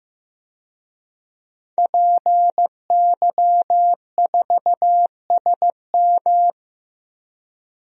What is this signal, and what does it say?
Morse code, heard as a keyed tone, about 15 wpm, 700 Hz. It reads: PY4SM
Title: DXCC-100 Call Signs - 5 Characters - Repeated 3x - 15wpm